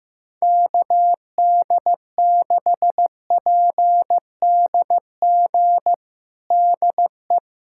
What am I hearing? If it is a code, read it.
Morse code, heard as a keyed tone, about 15 wpm, 700 Hz. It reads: KD6PDG DE
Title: USA Callsign DE Callsign 15wpm